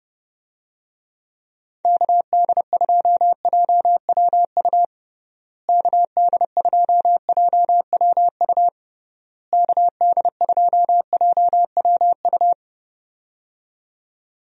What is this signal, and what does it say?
Morse code, heard as a keyed tone, about 30 wpm, 700 Hz. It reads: KD2JWU KD2JWU KD2JWU
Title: USA Call Signs - 6 Characters - Repeated 3x - 30wpm